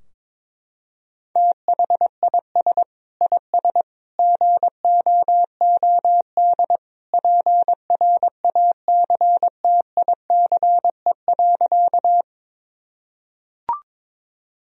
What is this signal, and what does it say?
Morse code, heard as a keyed tone, about 22 wpm, 700 Hz. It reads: THIS IS GOOD PRACTICE.
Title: Sentences from Top 1000 Words - Review Part 4 - 22wpm